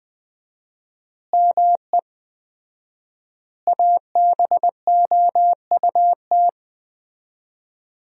Morse code, heard as a keyed tone, about 21 wpm, 700 Hz. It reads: ME ABOUT
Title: 4x Word-Spacing - Sentences from Top 200 Words - 20wpm